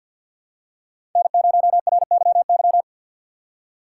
Morse code, heard as a keyed tone, about 51 wpm, 700 Hz. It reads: N0RYY
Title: USA Call Signs - 5 Characters - 3x Character Spacing - 50wpm